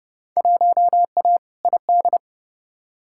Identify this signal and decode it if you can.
Morse code, heard as a keyed tone, about 31 wpm, 700 Hz. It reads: 1A ID
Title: Field Day Exchanges - 30wpm